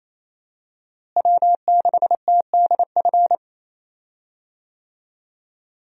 Morse code, heard as a keyed tone, about 27 wpm, 700 Hz. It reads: W6TDF
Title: USA Call Signs - 5 Characters - Repeated 3x - 28wpm